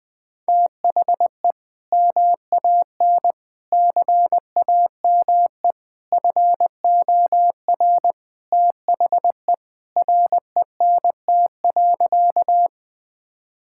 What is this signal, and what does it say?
Morse code, heard as a keyed tone, about 20 wpm, 700 Hz. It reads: THE MAN CAME FOR THE RENT.
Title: Sentences from Top 1000 Words - Review Part 5 - 20wpm